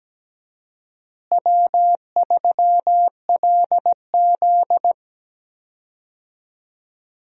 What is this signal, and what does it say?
Morse code, heard as a keyed tone, about 17 wpm, 700 Hz. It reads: W3LZ
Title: USA Call Signs - 4 Characters - Repeated 3x - 17wpm